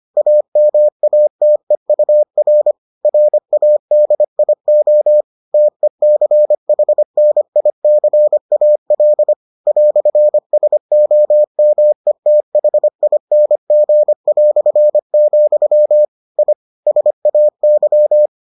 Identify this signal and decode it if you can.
Morse code, heard as a keyed tone, about 25 wpm, 600 Hz. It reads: AMATEUR RADIO TECHNICAL "SOMETHING", I SAY